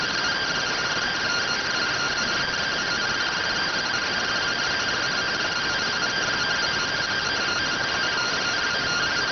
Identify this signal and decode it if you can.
Two-tone teleprinter signal, about 170 Hz shift.